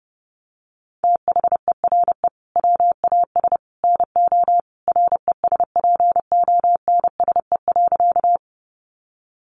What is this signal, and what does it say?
Morse code, heard as a keyed tone, about 30 wpm, 700 Hz. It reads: THERE WAS NO RESPONSE.
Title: Sentences from Top 1400 Words 30wpm